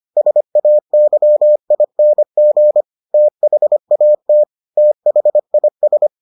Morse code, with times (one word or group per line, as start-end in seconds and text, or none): 0.2-2.8 SAYING
3.1-4.4 THAT
4.8-6.1 THIS